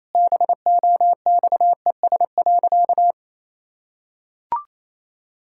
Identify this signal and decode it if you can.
Morse code, heard as a keyed tone, about 28 wpm, 700 Hz. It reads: BOXES.
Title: Pangram Challenge v2 - 28wpm